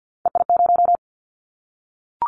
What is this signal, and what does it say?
Morse code, heard as a keyed tone, about 50 wpm, 700 Hz. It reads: EI0
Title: Callsign Prefixes for Top 50 DXCC - 50wpm